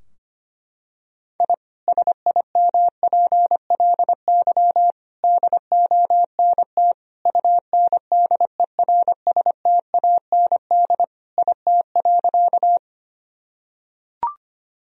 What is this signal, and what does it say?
Morse code, heard as a keyed tone, about 25 wpm, 700 Hz. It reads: I SIMPLY DONT UNDERSTAND IT.
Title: Sentences from Top 1000 Words - Review Part 4 - 25wpm